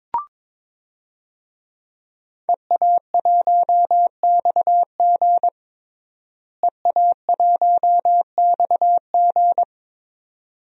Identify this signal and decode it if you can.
Morse code, heard as a keyed tone, about 21 wpm, 700 Hz. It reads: EA1XG EA1XG
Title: DXCC-100 Call Signs - 5 Characters - Repeated 2x - 22wpm